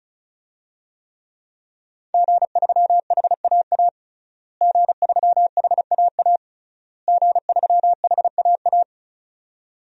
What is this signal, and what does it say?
Morse code, heard as a keyed tone, about 36 wpm, 700 Hz. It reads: G3HAA G3HAA G3HAA
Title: DXCC-100 Call Signs - 5 Characters - Repeated 3x - 35wpm